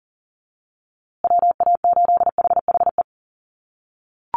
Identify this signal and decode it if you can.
Morse code, heard as a keyed tone, about 40 wpm, 700 Hz. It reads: WA8HHE
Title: USA Call Signs Repeated 2x 40wpm